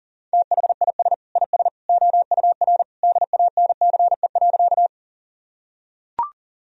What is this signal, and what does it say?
Morse code, heard as a keyed tone, about 40 wpm, 700 Hz. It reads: THIS IS OUR DANCE.
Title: Sentences from Top 1000 Words - Review Part 4 - 40wpm